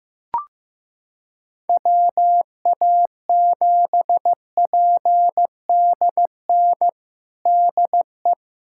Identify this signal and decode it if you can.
Morse code, heard as a keyed tone, about 15 wpm, 700 Hz. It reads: WA7PDN DE
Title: USA Callsign DE Callsign 15wpm